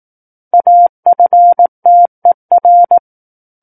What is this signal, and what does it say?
Morse code, heard as a keyed tone, about 18 wpm, 700 Hz. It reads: AFTER
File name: 18 WPM